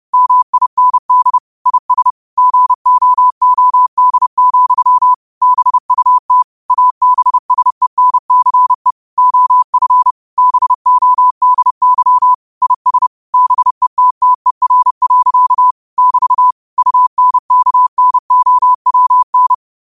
Morse code, heard as a keyed tone, about 30 wpm, 1000 Hz. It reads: MIND IS GOOD, BUT ABSENCE OF BODY IS BETTER. <BT> UNKNOWN
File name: qotdfiles / 30wpm